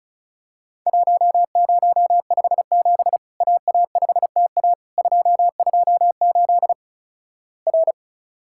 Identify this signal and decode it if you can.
Morse code, heard as a keyed tone, about 36 wpm, 700 Hz. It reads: 1057 AA5TA 228 R
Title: Worked All Europe DX Contest - QTC Exchange - 35wpm